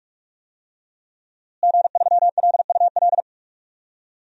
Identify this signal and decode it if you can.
Morse code, heard as a keyed tone, about 41 wpm, 700 Hz. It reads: G3LUL
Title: DXCC-100 Call Signs - 5 Characters - Repeated 3x - 45wpm